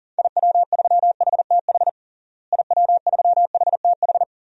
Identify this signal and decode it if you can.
Morse code, heard as a keyed tone, about 37 wpm, 700 Hz. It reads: IW3HTH IW3HTH
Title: DXCC-100 Call Signs - 6 Characters - Repeated 3x - 40wpm